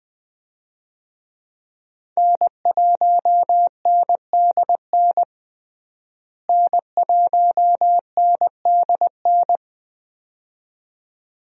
Morse code, heard as a keyed tone, about 21 wpm, 700 Hz. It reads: N1NDN N1NDN
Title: USA Call Signs - 5 Characters - Repeated 2x - 20wpm